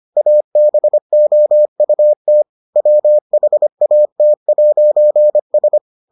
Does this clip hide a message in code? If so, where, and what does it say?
Yes: Morse code from 0.2 to 5.8 s, heard as a keyed tone, about 25 wpm, 600 Hz. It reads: ABOUT WHAT'S